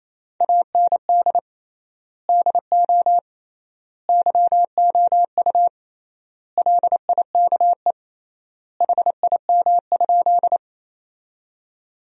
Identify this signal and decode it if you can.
Morse code, heard as a keyed tone, about 28 wpm, 700 Hz. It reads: AND DO YOU LIKE HIM?
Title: 3x Word-Spacing - Sentences from Top 100 Words - 28wpm